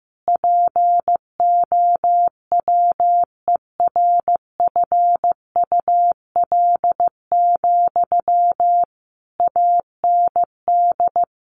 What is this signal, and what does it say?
Morse code, heard as a keyed tone, about 15 wpm, 700 Hz. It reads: POWERFUL, AND
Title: Sentences from Top 1500 Words 15wpm